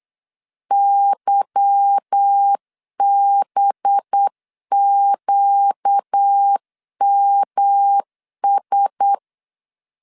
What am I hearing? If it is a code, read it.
Morse code, heard as a keyed tone, about 8 wpm, 800 Hz. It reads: YBQMS
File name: LEZIONI / 8 YJ45 / 8 YJ4 8wpm 3246